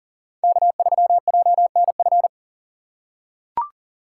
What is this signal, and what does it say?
Morse code, heard as a keyed tone, about 40 wpm, 700 Hz. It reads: K3JNF
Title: USA Call Signs - 5 Characters - 6x Character Spacing - 40wpm